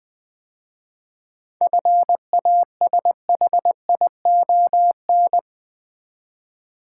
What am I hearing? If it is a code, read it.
Morse code, heard as a keyed tone, about 20 wpm, 690 Hz. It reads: FASHION
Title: 7 Letter Words - First 3 Letters Given - 20wpm